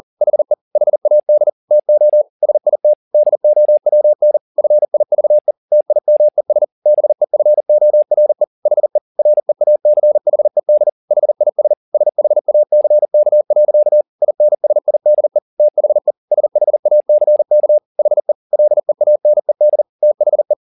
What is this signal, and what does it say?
Morse code, heard as a keyed tone, about 40 wpm, 600 Hz. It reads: HE HAD TO SIT DOWN FIVE TIMES BEFORE HE REACHED HIS SHACK. INSIDE THE SHACK HE LEANED THE
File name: the old man and the sea 2 40wpm